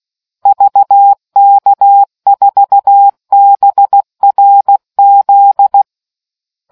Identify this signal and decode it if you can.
Morse code, heard as a keyed tone, about 16 wpm, 800 Hz. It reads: VK4BRZ